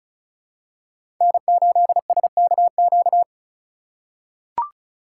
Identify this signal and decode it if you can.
Morse code, heard as a keyed tone, about 36 wpm, 700 Hz. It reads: N8SKQ
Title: USA Call Signs - 5 Characters - 2x Character Spacing - 35wpm